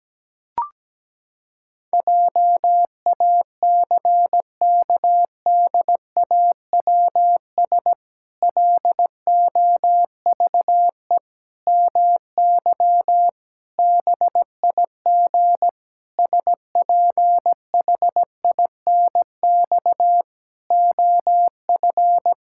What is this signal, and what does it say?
Morse code, heard as a keyed tone, about 17 wpm, 700 Hz. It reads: JACKDAWS LOVE MY BIG SPHINX OF
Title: Pangram Challenge v2 - 17wpm